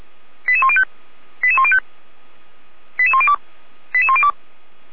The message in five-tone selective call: 88105 88105 88102 88102

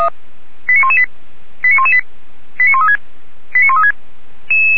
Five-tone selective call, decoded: 89118 89118 89136 89136

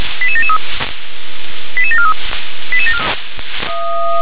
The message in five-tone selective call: 88773 88743

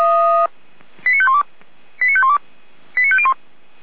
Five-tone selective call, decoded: 89512 89512 89501

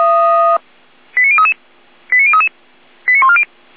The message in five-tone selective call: 89033 89033 89140

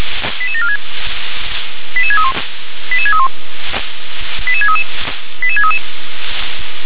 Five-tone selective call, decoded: 88746 88631 88631 88633 88633

